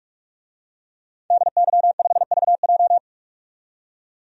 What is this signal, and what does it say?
Morse code, heard as a keyed tone, about 44 wpm, 700 Hz. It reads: DY5VJ
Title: DXCC-100 Call Signs - 5 Characters - 45wpm